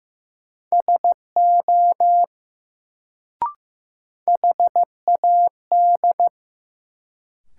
Morse code, heard as a keyed tone, about 15 wpm, 700 Hz. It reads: SO HAD
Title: Sentences from Top 100 Words - New Word Drill - 15wpm